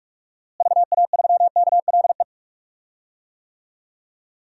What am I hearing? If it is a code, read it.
Morse code, heard as a keyed tone, about 46 wpm, 700 Hz. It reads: VA3KLE
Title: DXCC-100 Call Signs - 6 Characters - Repeated 3x - 45wpm